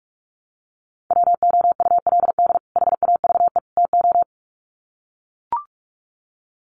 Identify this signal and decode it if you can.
Morse code, heard as a keyed tone, about 44 wpm, 700 Hz. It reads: WOULD HAVE TO
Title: Sets of 3 Words 45wpm